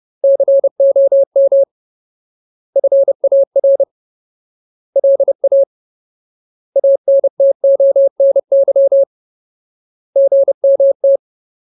Morse code, heard as a keyed tone, about 30 wpm, 550 Hz. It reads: COM FAR LA ANTONY GMT
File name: Ultimate Word List / 30WPM Ultimate